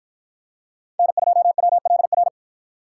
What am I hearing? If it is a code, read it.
Morse code, heard as a keyed tone, about 50 wpm, 700 Hz. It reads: N2WLR